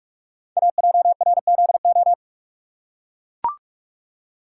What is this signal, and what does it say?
Morse code, heard as a keyed tone, about 44 wpm, 700 Hz. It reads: AJRZO
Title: Code Groups - 5 Letter:Numbers - 45wpm